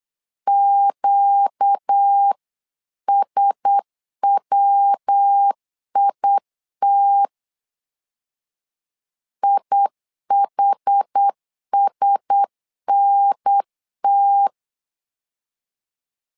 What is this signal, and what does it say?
Morse code, heard as a keyed tone, about 8 wpm, 800 Hz. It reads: QSWIT IHSNT
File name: LEZIONI / 4 Q1S 8wpm 2710